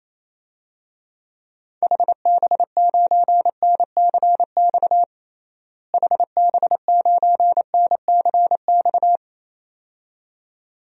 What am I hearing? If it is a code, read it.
Morse code, heard as a keyed tone, about 28 wpm, 700 Hz. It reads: HB9NCX HB9NCX
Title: DXCC-100 Call Signs - 6 Characters - Repeated 2x - 28wpm